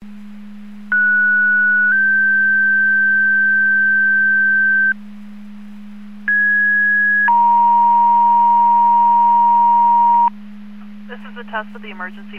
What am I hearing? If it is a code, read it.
Two-tone page, heard as a one-second tone, then a three-second tone, about 1470 then 1550 Hz, about 1650 then 940 Hz, with speech. It is heard from 0.9 to 10.3 s.